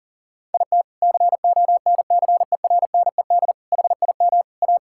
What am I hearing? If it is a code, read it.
Morse code, heard as a keyed tone, about 37 wpm, 700 Hz. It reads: IT CONCERNED HIM A
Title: Sentences from Top 1000 Words - Review Part 6 - 40wpm